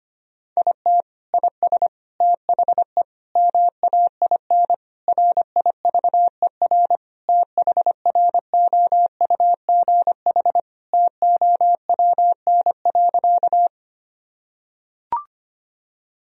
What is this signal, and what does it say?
Morse code, heard as a keyed tone, about 25 wpm, 700 Hz. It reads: IT IS THE MAIN RIVER THROUGH TOWN.
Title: Sentences from Top 1000 Words - Review Part 2 - 25wpm